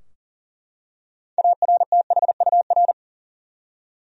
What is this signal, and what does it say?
Morse code, heard as a keyed tone, about 40 wpm, 700 Hz. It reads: ARTHUR
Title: USA Common Names - 40wpm